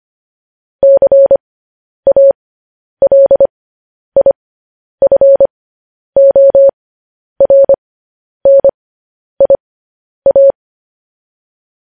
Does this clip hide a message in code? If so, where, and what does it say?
Yes: Morse code from 0.8 to 10.5 s, heard as a keyed tone, about 25 wpm, spaced as for 10 wpm, 550 Hz. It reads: CALIFORNIA